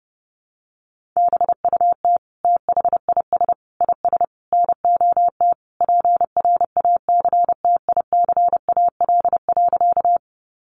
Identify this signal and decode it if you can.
Morse code, heard as a keyed tone, about 30 wpm, 700 Hz. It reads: BUT THIS IS NOT PRACTICAL.
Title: Sentences from Top 1400 Words 30wpm